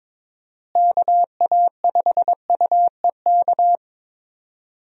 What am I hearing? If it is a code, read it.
Morse code, heard as a keyed tone, about 22 wpm, 700 Hz. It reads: KA5UEK
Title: USA Call Signs - 6 Characters - Repeated 3x - 22wpm